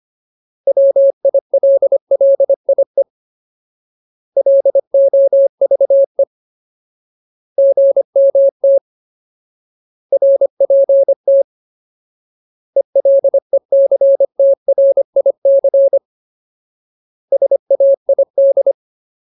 Morse code, heard as a keyed tone, about 25 wpm, 550 Hz. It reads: WILLIE LOVE GMT RPT ELECTRIC SAID
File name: Ultimate Word List / 25WPM Ultimate